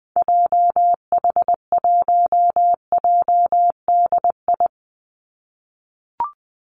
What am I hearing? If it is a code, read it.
Morse code, heard as a keyed tone, about 20 wpm, 700 Hz. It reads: JH1JDI
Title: Call Signs - 20wpm